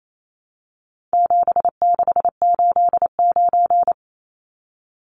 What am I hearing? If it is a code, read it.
Morse code, heard as a keyed tone, about 28 wpm, 700 Hz. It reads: 7689
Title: Numbers - 4-Digit 28wpm